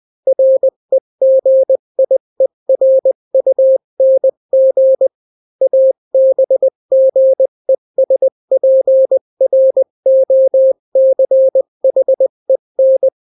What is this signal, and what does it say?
Morse code, heard as a keyed tone, about 20 wpm, 530 Hz. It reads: REGIERUNG ABGESPROCHEN
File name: daily / 18wpm